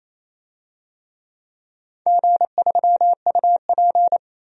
Morse code, heard as a keyed tone, about 28 wpm, 700 Hz. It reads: G3UP
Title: DXCC-100 Call Signs - 4 Characters - Repeated 2x - 28wpm